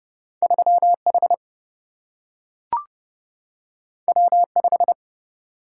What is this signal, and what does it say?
Morse code, heard as a keyed tone, about 29 wpm, 700 Hz. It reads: 3H W5
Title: Single-Letter Single-Number 30wpm